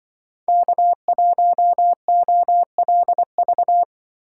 Morse code, heard as a keyed tone, about 24 wpm, 700 Hz. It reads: K1OLV
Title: Speed Racing - USA 1x3 Call Signs 20wpm